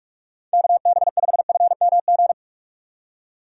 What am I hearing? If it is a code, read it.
Morse code, heard as a keyed tone, about 46 wpm, 700 Hz. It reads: KB5FMG
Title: USA Call Signs - 6 Characters - 3x Character Spacing - 45wpm